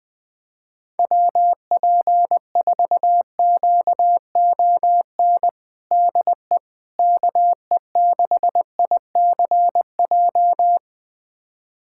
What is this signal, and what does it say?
Morse code, heard as a keyed tone, about 21 wpm, 700 Hz. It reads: WP4QON DE KE6ICJ
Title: USA Callsign DE Callsign 20wpm